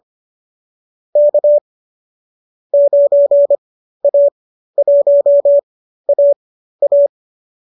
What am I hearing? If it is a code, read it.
Morse code, heard as a keyed tone, about 25 wpm, spaced as for 13 wpm, 570 Hz. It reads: K 9A1AA
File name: qso219 13